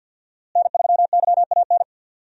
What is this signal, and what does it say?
Morse code, heard as a keyed tone, about 45 wpm, 700 Hz. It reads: N3YAN
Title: USA Call Signs - 5 Characters - Repeated 3x - 50wpm